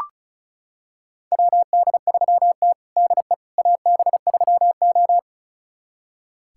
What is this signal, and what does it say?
Morse code, heard as a keyed tone, about 36 wpm, 700 Hz. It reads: WD3T DE AB3O
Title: USA Callsign DE Callsign - 4 Characters - 35wpm